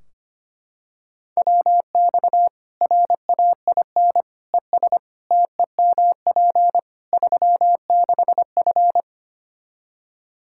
Morse code, heard as a keyed tone, about 25 wpm, 700 Hz. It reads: WX RAIN ES TEMP 36F
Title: Practice QSO - Exchange 2 - Info - 25wpm